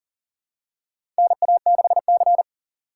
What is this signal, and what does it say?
Morse code, heard as a keyed tone, about 40 wpm, 700 Hz. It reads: NA6C
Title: USA Call Signs - 4 Characters - 2x Character Spacing - 40wpm